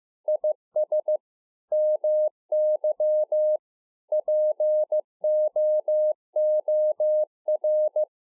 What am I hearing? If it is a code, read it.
Morse code, heard as a keyed tone, about 15 wpm, 610 Hz. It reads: IS MY POOR
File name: 15WPM